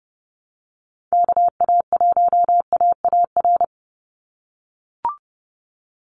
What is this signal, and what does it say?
Morse code, heard as a keyed tone, about 28 wpm, 700 Hz. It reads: KA1AAR
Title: USA Call Signs 30wpm